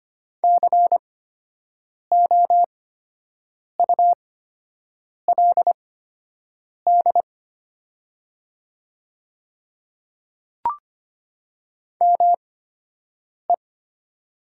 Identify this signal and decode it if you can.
Morse code, heard as a keyed tone, about 25 wpm, spaced as for 7 wpm, 700 Hz. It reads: COULD ME
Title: 8x Character Spacing - Top 200 Words - 25wpm